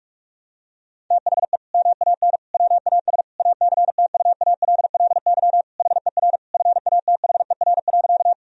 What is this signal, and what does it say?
Morse code, heard as a keyed tone, about 42 wpm, 700 Hz. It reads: THE MAN WAS ACTUALLY HER FATHER.
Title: Sentences from Top 1000 Words - Review Part 4 - 45wpm